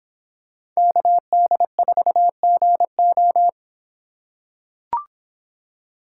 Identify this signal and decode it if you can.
Morse code, heard as a keyed tone, about 26 wpm, 700 Hz. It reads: KD4GO